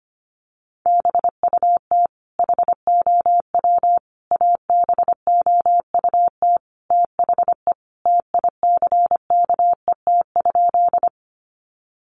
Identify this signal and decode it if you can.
Morse code, heard as a keyed tone, about 25 wpm, 700 Hz. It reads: BUT HOW ABOUT THE TICKET?
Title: Sentences from Top 1400 Words 25wpm